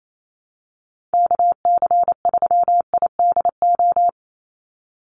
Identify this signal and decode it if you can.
Morse code, heard as a keyed tone, about 28 wpm, 700 Hz. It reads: KC3IDO
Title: USA Call Signs 28wpm